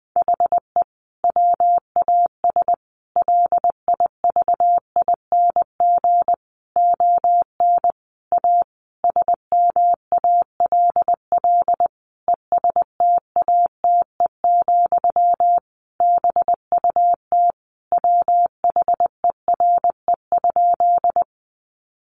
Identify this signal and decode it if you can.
Morse code, heard as a keyed tone, about 20 wpm, 700 Hz. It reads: HE WAS LIVING ON A SMALL ESTATE, BUT WHERE?
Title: Sentences from Top 1200 Words 20wpm